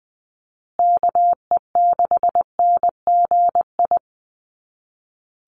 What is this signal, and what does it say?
Morse code, heard as a keyed tone, about 20 wpm, 700 Hz. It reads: KE6NGI
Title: USA Call Signs Repeated 2x 20wpm